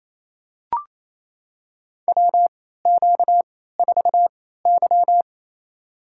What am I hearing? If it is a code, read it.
Morse code, heard as a keyed tone, about 28 wpm, spaced as for 16 wpm, 700 Hz. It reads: WQ4Y
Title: USA Call Signs - 3x Character Spacing - 28wpm